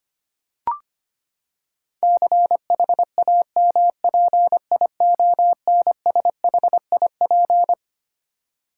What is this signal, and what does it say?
Morse code, heard as a keyed tone, about 25 wpm, 700 Hz. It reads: CHAMPIONSHIP
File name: Twelve Letter Words - 25wpm